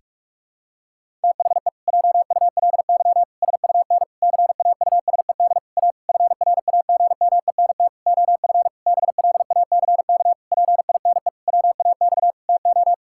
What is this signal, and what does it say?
Morse code, heard as a keyed tone, about 48 wpm, 700 Hz. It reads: THE JULY SUN CAUSED A FRAGMENT OF BLACK PINE WAX TO
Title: Pangram Challenge v2 - 45wpm